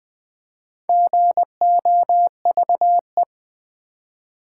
Morse code, heard as a keyed tone, about 20 wpm, 700 Hz. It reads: GOVE
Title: Build A Word - 4 Letters At A Time - 10-Letter Words - 20wpm